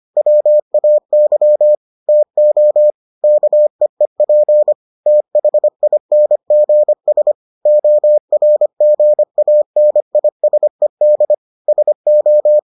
Morse code, heard as a keyed tone, about 25 wpm, 600 Hz. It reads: WAY TO KEEP THINGS ORGANISED SO